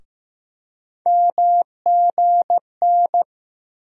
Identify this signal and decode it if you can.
Morse code, heard as a keyed tone, about 15 wpm, 700 Hz. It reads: MGN
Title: USA Call Sign Postfixes - 3 Letters 15wpm